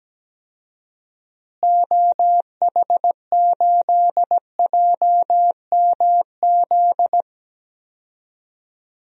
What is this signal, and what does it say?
Morse code, heard as a keyed tone, about 17 wpm, 700 Hz. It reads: OH8JMZ
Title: DXCC-100 Call Signs - 6 Characters - Repeated 2x - 17wpm